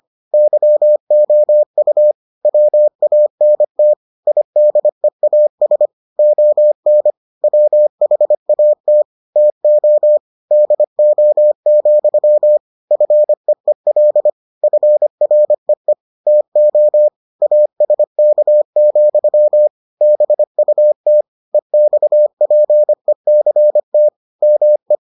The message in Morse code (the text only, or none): YOU WANT IDEAS ON WHAT TO DO, FEEL FREE TO ASK, BUT EXPECT ME